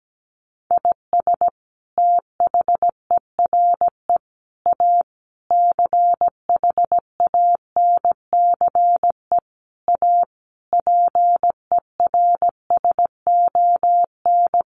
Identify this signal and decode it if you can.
Morse code, heard as a keyed tone, about 17 wpm, 700 Hz. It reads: IS THERE A CHANCE A PERSON
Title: Sentences from Top 900 Words 17wpm